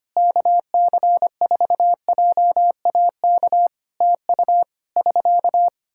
Morse code, heard as a keyed tone, about 25 wpm, 700 Hz. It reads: KC4JAK TU <SK>
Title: Practice QSO - Exchange 4 - Ending - 25wpm